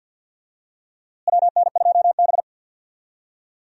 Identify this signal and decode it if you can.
Morse code, heard as a keyed tone, about 46 wpm, 700 Hz. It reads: WN2B
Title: USA Call Signs - 4 Characters - Repeated 3x - 50wpm